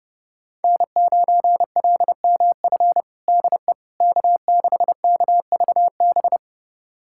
Morse code, heard as a keyed tone, about 30 wpm, 700 Hz. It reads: N9LMF DE K6KVB
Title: USA Callsign DE Callsign - 5 Characters - 30wpm